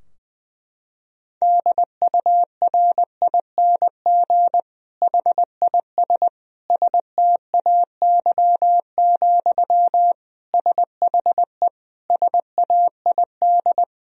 Morse code, heard as a keyed tone, about 20 wpm, 700 Hz. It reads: DURING HIS STAY, SHE SAID
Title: Sentences from Top 1000 Words - Review Part 2 - 20wpm